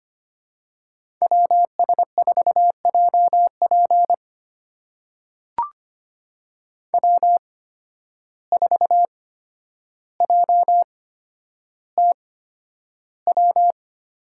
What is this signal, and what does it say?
Morse code, heard as a keyed tone, about 25 wpm, spaced as for 11 wpm, 700 Hz. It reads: WS4JP W4JTW
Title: USA Call Signs - 5 Characters - 8x Character Spacing - 25wpm